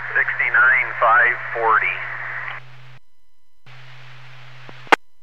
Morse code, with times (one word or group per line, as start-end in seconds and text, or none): none